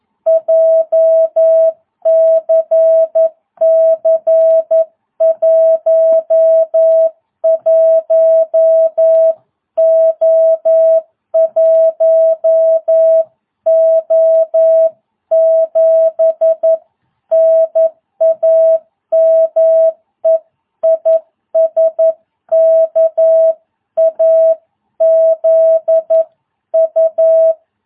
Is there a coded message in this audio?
Yes: Morse code from 0.3 to 27.5 s, heard as a keyed tone, about 11 wpm, 650 Hz. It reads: JCC 11O1O7 NAME IS KAZU